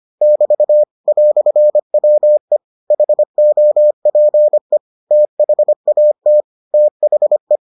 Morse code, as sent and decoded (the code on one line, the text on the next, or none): -...- / .-..-. .-- . / .... --- .--. / - .... .- - / - .... .
<BT> "WE HOPE THAT THE